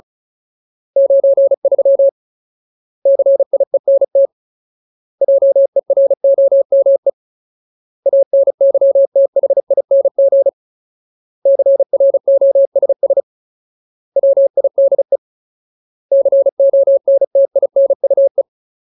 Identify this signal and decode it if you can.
Morse code, heard as a keyed tone, about 34 wpm, 550 Hz. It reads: 93 CIENT JEROME ANYTHING CROSS WIDE CONTINUE
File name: Ultimate Word List / 35WPM Ultimate